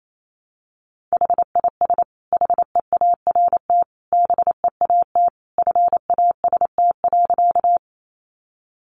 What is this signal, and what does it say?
Morse code, heard as a keyed tone, about 28 wpm, 700 Hz. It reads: HIS HEART BEAT FAST.
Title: Sentences from Top 1300 Words 28wpm